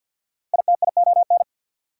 Morse code, heard as a keyed tone, about 54 wpm, 690 Hz. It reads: ITION
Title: Prefixes and Suffixes 50wpm